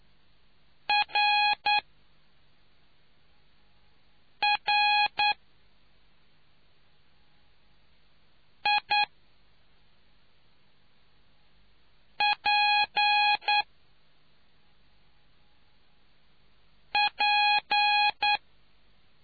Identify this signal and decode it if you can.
Morse code, heard as a keyed tone, about 9 wpm, spaced as for 3 wpm, 830 Hz. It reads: RRIPP